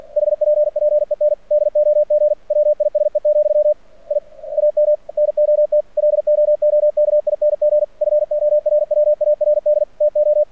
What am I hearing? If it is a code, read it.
Morse code, heard as a keyed tone, about 48 wpm, 600 Hz. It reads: HPPEN DOG WISE, I AM NOT LOOKING FORWARD TO
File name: archived / CRSnet